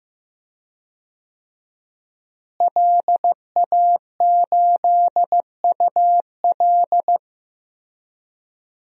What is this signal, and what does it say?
Morse code, heard as a keyed tone, about 15 wpm, 700 Hz. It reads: LA8UL
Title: DXCC-100 Call Signs - 5 Characters - Repeated 2x - 15wpm